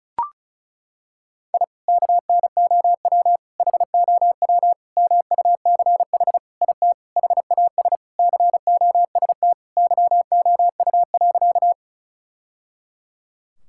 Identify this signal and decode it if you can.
Morse code, heard as a keyed tone, about 34 wpm, 700 Hz. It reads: I KNOW HOW MUCH IT HAS COST YOU.
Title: Sentences from Top 1000 Words - Review Part 5 - 35wpm